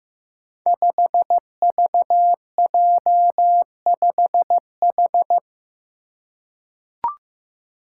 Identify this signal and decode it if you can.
Morse code, heard as a keyed tone, about 15 wpm, 700 Hz. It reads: 5VJ5H
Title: Five Challenging Letter-Number Combinations 15wpm